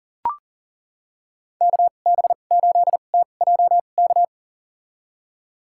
Morse code, heard as a keyed tone, about 40 wpm, spaced as for 29 wpm, 700 Hz. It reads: KB8TJK
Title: USA Call Signs - 2x Character Spacing - 40wpm